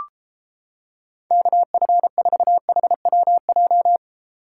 Morse code, heard as a keyed tone, about 29 wpm, 700 Hz. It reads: KF4HWJ